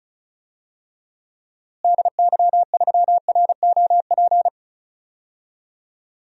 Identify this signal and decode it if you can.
Morse code, heard as a keyed tone, about 34 wpm, 700 Hz. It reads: DY3ROP
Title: DXCC-100 Call Signs - 6 Characters - Repeated 2x - 35wpm